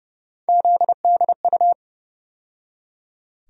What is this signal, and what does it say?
Morse code, heard as a keyed tone, about 32 wpm, 700 Hz. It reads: ZDU
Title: USA Call Sign Postfixes - 3 Letters 30wpm